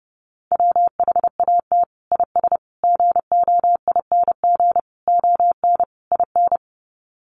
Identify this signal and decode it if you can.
Morse code, heard as a keyed tone, about 30 wpm, 700 Hz. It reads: WHAT IS GOING ON IN
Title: Sets of 5 Words Encore2 30wpm